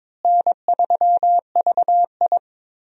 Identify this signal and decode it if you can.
Morse code, heard as a keyed tone, about 22 wpm, 700 Hz. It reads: N3VI
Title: USA Call Signs - 4 Characters - Repeated 2x - 22wpm